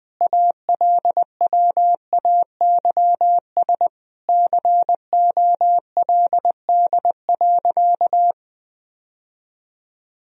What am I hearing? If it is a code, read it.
Morse code, heard as a keyed tone, about 19 wpm, 700 Hz. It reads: ALWAYS COLD.
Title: Sentences from Top 1000 Words - Review Part 1 - 20wpm